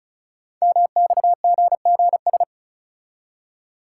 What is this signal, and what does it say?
Morse code, heard as a keyed tone, about 36 wpm, 690 Hz. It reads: MXGGS
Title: Code Groups - 5 Letters - 35wpm